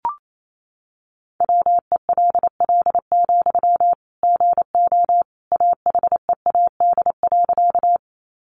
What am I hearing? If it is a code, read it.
Morse code, heard as a keyed tone, about 28 wpm, 700 Hz. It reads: WELL, GO AHEAD.
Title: Sentences from Top 1100 Words 28wpm